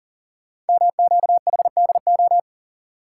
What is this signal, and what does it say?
Morse code, heard as a keyed tone, about 40 wpm, 700 Hz. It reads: MQHDO
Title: Code Groups - 5 Letters - 40wpm